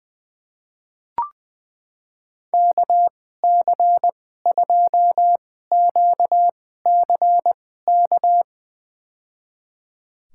Morse code, heard as a keyed tone, about 20 wpm, spaced as for 15 wpm, 700 Hz. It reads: KC2QCK
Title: USA Call Signs - 6 Characters - 2x Character Spacing - 20wpm